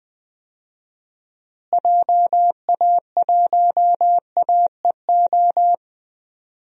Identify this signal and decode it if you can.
Morse code, heard as a keyed tone, about 20 wpm, 700 Hz. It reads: JA1AEO
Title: DXCC-100 Call Signs - 6 Characters - Repeated 2x - 20wpm